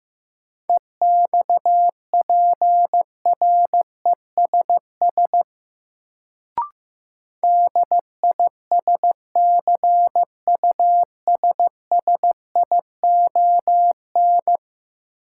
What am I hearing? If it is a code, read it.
Morse code, heard as a keyed tone, about 15 wpm, 700 Hz. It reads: EXPRESS DISCUSSION
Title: Sentences from Top 1100 Words - New Word Drill Alt - 15wpm